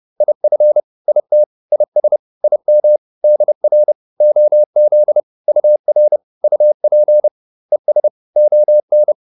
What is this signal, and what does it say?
Morse code, heard as a keyed tone, about 30 wpm, 600 Hz. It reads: IF IT IS IM DR OZ UR UP ES ON